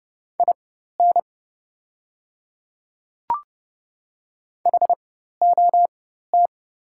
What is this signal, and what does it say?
Morse code, heard as a keyed tone, about 30 wpm, spaced as for 14 wpm, 700 Hz. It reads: IN HOT
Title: 4x Character Spacing - Top 100 Words - 30wpm